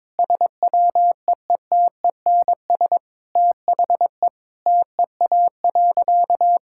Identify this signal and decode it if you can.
Morse code, heard as a keyed tone, about 22 wpm, 700 Hz. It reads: SWEETENS THE TEA.
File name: practice / 22wpm / Sentences with 2-Word Phrases - 22wpm